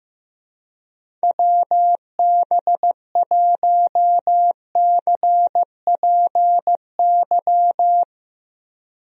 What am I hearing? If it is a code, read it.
Morse code, heard as a keyed tone, about 15 wpm, 700 Hz. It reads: WB1CPY
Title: USA Call Signs - 6 Characters - Repeated 2x - 15wpm